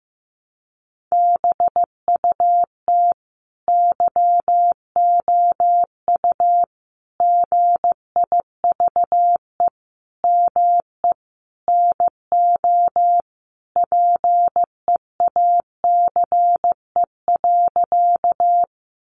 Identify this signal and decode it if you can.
Morse code, heard as a keyed tone, about 15 wpm, 700 Hz. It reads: BUT YOU GIVE ME NO PEACE.
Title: Sentences from Top 1500 Words 15wpm